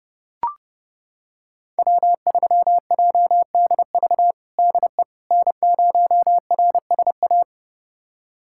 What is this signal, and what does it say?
Morse code, heard as a keyed tone, about 30 wpm, 700 Hz. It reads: W3JDV DE N0RSA
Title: USA Callsign DE Callsign - 5 Characters - 30wpm